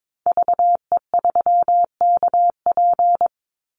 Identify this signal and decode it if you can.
Morse code, heard as a keyed tone, about 22 wpm, 690 Hz. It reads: VE3KP
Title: Call Signs Repeated 3x - 22wpm